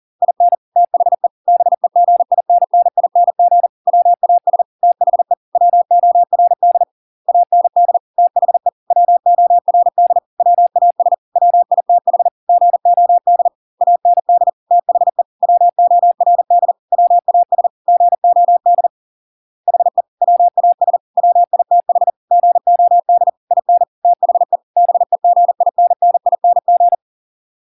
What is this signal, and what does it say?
Morse code, heard as a keyed tone, about 40 wpm, 700 Hz. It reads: IN THE BEGINNING WAS THE WORD AND THE WORD WAS WITH GOD AND THE WORD WAS GOD HE WAS WITH GOD IN THE BEGINNING